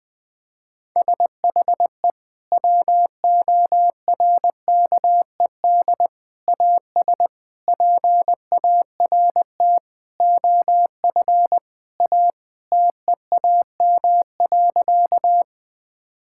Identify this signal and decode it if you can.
Morse code, heard as a keyed tone, about 20 wpm, 700 Hz. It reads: SHE WORKED AS PART OF A TEAM.
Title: Sentences with 4-Word Phrases - 20wpm